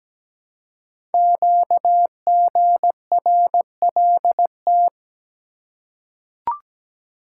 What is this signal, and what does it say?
Morse code, heard as a keyed tone, about 17 wpm, 700 Hz. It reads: QGRLT
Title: Code Groups - 5 Letters - 17wpm